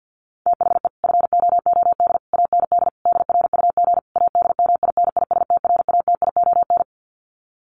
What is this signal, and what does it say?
Morse code, heard as a keyed tone, about 50 wpm, 700 Hz. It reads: THE FOOD AND DRUG ADMINISTRATION
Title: Sets of 5 Words 50wpm